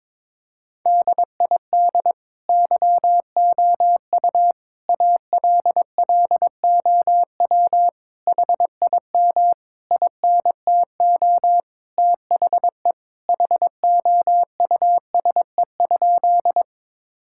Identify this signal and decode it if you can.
Morse code, heard as a keyed tone, about 22 wpm, 700 Hz. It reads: DID YOU ALLOW HIM INTO THE HOUSE?
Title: Sentences from Top 1000 Words - Review Part 5 - 22wpm